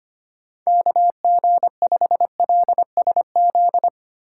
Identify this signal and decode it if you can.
Morse code, heard as a keyed tone, about 31 wpm, 700 Hz. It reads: KG5LSZ
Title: Speed Racing - USA 2x3 Call Signs 25wpm